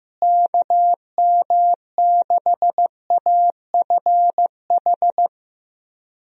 Morse code, heard as a keyed tone, about 15 wpm, 700 Hz. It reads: KM6AFH
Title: USA Call Signs - 6 Characters - Repeated 2x - 15wpm